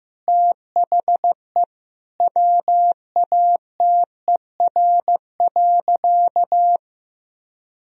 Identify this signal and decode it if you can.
Morse code, heard as a keyed tone, about 15 wpm, 700 Hz. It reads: THE WATER.
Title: Sentences from Top 1000 Words - Review Part 2 - 15wpm